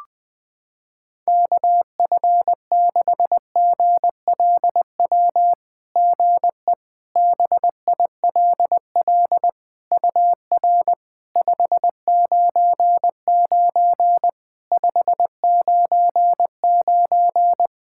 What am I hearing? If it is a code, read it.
Morse code, heard as a keyed tone, about 20 wpm, 660 Hz. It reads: KF6GLW GE BILL UR 599 599
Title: SOTA QSO Practice 20wpm